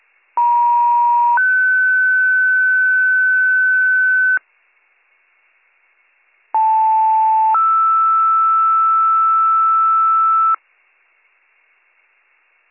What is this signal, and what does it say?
Two-tone page, heard as a one-second tone, then a three-second tone, about 950 then 1500 Hz, about 880 then 1320 Hz. It is heard from 0.4 to 10.6 s.